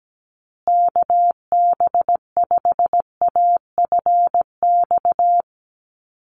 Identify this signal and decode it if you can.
Morse code, heard as a keyed tone, about 17 wpm, 700 Hz. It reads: KB5AFX
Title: USA Call Signs Repeated 3x 17wpm